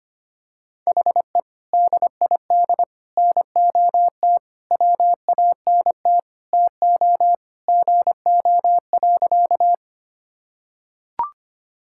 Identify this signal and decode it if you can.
Morse code, heard as a keyed tone, about 25 wpm, 700 Hz. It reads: HE DID NOT WANT TO GO.
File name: practice / 25wpm / Sentences with 4-Word Phrases - 25wpm